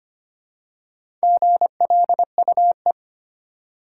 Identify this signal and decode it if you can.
Morse code, heard as a keyed tone, about 25 wpm, 700 Hz. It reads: GLUE
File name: practice / 25wpm / Easy Words with Hints - 25wpm